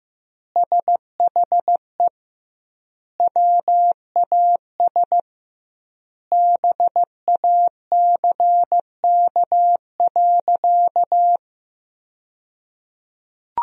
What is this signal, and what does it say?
Morse code, heard as a keyed tone, about 15 wpm, 700 Hz. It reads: SHE WAS BACK.
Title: 2x Word-Spacing - Sentences from Top 500 Words - 15wpm